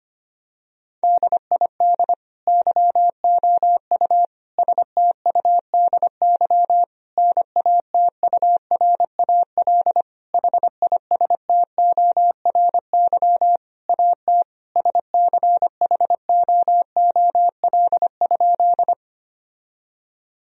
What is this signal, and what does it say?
Morse code, heard as a keyed tone, about 25 wpm, 700 Hz. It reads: DID YOU STUDY NATURAL HISTORY AT SCHOOL?
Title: Sentences from Top 1000 Words - Review Part 4 - 25wpm